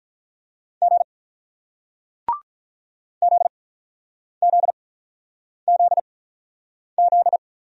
Morse code, heard as a keyed tone, about 47 wpm, spaced as for 9 wpm, 700 Hz. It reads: G ZZZZ